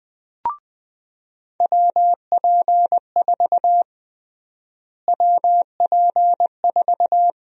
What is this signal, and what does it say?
Morse code, heard as a keyed tone, about 20 wpm, 700 Hz. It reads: WP4 WP4
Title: USA Call Sign Prefixes - Repeated 2x - 20wpm